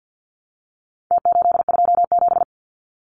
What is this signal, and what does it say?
Morse code, heard as a keyed tone, about 50 wpm, 700 Hz. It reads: T827
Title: Numbers - Contest Style - 4-Digit 50wpm